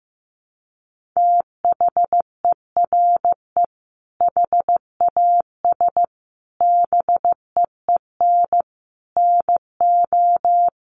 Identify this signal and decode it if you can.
Morse code, heard as a keyed tone, about 15 wpm, 700 Hz. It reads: THERE HAS BEEN NO
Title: Sentences from Top 1500 Words 15wpm